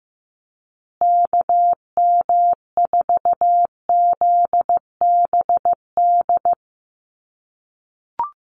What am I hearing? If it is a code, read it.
Morse code, heard as a keyed tone, about 15 wpm, 700 Hz. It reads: KM4ZBD
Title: USA Call Signs Repeated 3x 15wpm